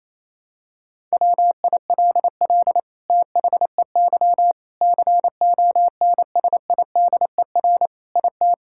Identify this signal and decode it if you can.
Morse code, heard as a keyed tone, about 28 wpm, 700 Hz. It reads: WILL THEY CONSIDER IT
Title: Sentences from Top 1000 Words - Review Part 5 - 28wpm